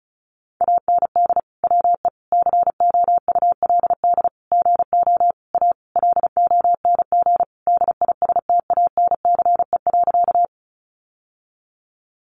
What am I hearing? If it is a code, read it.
Morse code, heard as a keyed tone, about 34 wpm, 700 Hz. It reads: AND WE COULD GO A LONG DISTANCE.
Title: Sentences from Top 1200 Words 35wpm